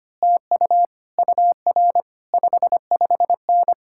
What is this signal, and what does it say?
Morse code, heard as a keyed tone, about 25 wpm, 700 Hz. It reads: TU UR 55N
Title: Speed-Racing Common QSO Phrases - 25wpm